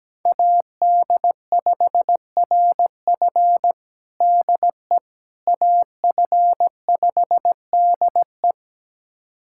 Morse code, heard as a keyed tone, about 17 wpm, 700 Hz. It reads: AD5RF DE AF5DE
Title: USA Callsign DE Callsign - 5 Characters - 17wpm